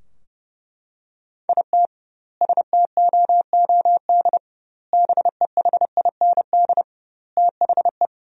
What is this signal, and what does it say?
Morse code, heard as a keyed tone, about 32 wpm, 700 Hz. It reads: IT STOOD BEHIND THE
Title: 2x Word-Spacing - Sentences from Top 500 Words - 30wpm